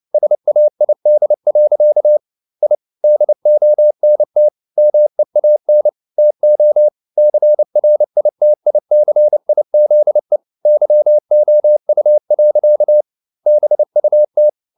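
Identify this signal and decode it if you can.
Morse code, heard as a keyed tone, about 29 wpm, 600 Hz. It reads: SAID. I DONT MEAN TO CRITICIZE YOU. BUT
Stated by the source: Morse code at 29 wpm, sending Ken Follett's novel Never in English, from CW med Gnister.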